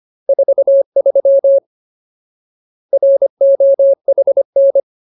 Morse code, heard as a keyed tone, about 25 wpm, 550 Hz. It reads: 43 ROHN
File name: Ultimate Word List / 25WPM Ultimate